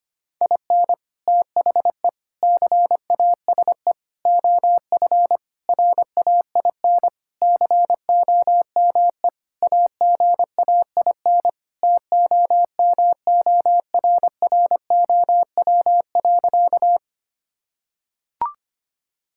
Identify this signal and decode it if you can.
Morse code, heard as a keyed tone, about 25 wpm, 700 Hz. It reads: IN THE CASE OF RAIN COME AGAIN TOMORROW.
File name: practice / 25wpm / Sentences with 4-Word Phrases - 25wpm